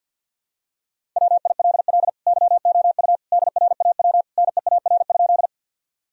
Morse code, heard as a keyed tone, about 48 wpm, 700 Hz. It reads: WILL YOU DRAW NEAR?
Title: Sentences from Top 1000 Words - Review Part 1 - 50wpm